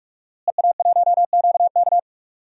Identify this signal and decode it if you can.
Morse code, heard as a keyed tone, about 46 wpm, 700 Hz. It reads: EA1QK
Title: DXCC-100 Call Signs - 5 Characters - Repeated 3x - 45wpm